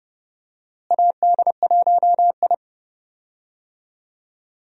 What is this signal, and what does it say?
Morse code, heard as a keyed tone, about 32 wpm, 700 Hz. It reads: AD1I
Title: USA Call Signs - 4 Characters - Repeated 2x - 30wpm